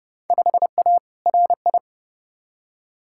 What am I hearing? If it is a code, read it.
Morse code, heard as a keyed tone, about 32 wpm, 700 Hz. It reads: 5A RI
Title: Field Day Exchanges Encore - 30wpm